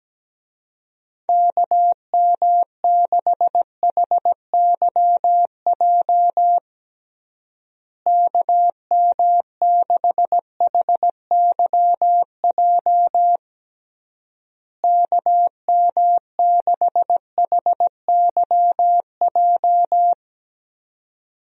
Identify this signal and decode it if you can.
Morse code, heard as a keyed tone, about 17 wpm, 700 Hz. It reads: KM6HYJ KM6HYJ KM6HYJ
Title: USA Call Signs - 6 Characters - Repeated 3x - 17wpm